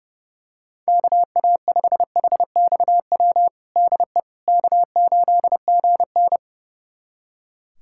Morse code, heard as a keyed tone, about 30 wpm, 700 Hz. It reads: KA5HXW DE K8GN
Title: USA Callsign DE Callsign 30wpm